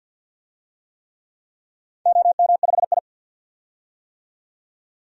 Morse code, heard as a keyed tone, about 53 wpm, 700 Hz. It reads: OM5I
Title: DXCC-100 Call Signs - 4 Characters - 50wpm